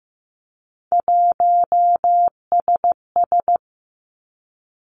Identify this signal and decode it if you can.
Morse code, heard as a keyed tone, about 15 wpm, 700 Hz. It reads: 1SS
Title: Three Challenging Letter-Number Combinations 15wpm